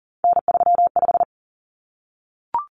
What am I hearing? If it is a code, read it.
Morse code, heard as a keyed tone, about 40 wpm, 700 Hz. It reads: N35
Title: Common Cut Numbers - 3-Digit 40wpm